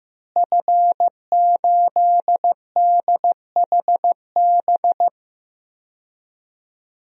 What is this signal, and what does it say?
Morse code, heard as a keyed tone, about 15 wpm, 700 Hz. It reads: F8DHB
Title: DXCC-100 Call Signs - 5 Characters - Repeated 3x - 15wpm